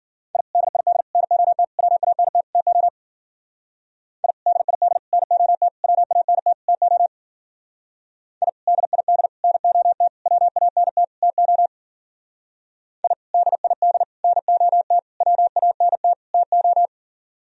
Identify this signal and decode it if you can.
Morse code, heard as a keyed tone, about 49 wpm, 700 Hz. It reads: I DID NOT WANT TO I DID NOT WANT TO I DID NOT WANT TO I DID NOT WANT TO
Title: Speed-Racing - Sets of 5 Words 40wpm